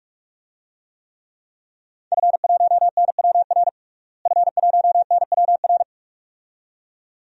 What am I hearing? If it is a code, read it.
Morse code, heard as a keyed tone, about 43 wpm, 700 Hz. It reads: F1NWR F1NWR
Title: DXCC-100 Call Signs - 5 Characters - Repeated 2x - 45wpm